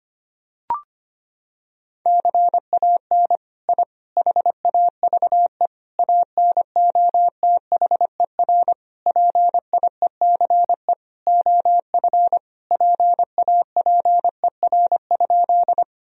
Morse code, heard as a keyed tone, about 25 wpm, 700 Hz. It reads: CAN I HAVE ANOTHER PIECE OF PAPER?
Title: Sentences from Top 1000 Words - Review Part 2 - 25wpm